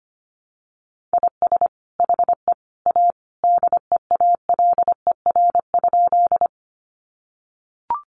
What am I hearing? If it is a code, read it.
Morse code, heard as a keyed tone, about 25 wpm, 700 Hz. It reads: IS HE A DEALER?
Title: Sentences from Top 1100 Words 25wpm